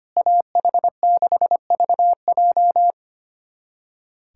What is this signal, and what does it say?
Morse code, heard as a keyed tone, about 25 wpm, 700 Hz. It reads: AH6VJ
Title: USA Callsign DE Callsign - 5 Characters - 25wpm